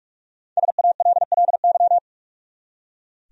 Morse code, heard as a keyed tone, about 44 wpm, 700 Hz. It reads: SALLY
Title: USA Common Names - Max 5 Letters - 45wpm